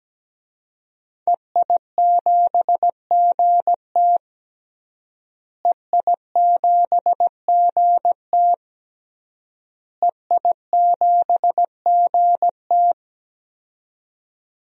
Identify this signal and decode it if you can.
Morse code, heard as a keyed tone, about 17 wpm, 700 Hz. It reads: EI7GT EI7GT EI7GT
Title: DXCC-100 Call Signs - 5 Characters - Repeated 3x - 17wpm